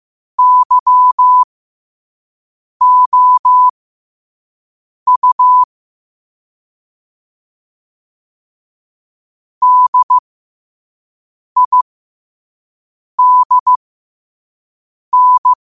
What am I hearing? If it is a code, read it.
Morse code, heard as a keyed tone, about 15 wpm, spaced as for 5 wpm, 1000 Hz. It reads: YOU DIDN
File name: qotdfiles / 5wpm